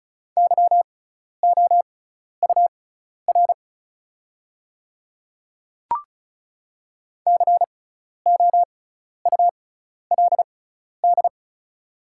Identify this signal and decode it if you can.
Morse code, heard as a keyed tone, about 36 wpm, spaced as for 12 wpm, 700 Hz. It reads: YOUR COULD
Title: 6x Character Spacing - Top 200 Words - 35wpm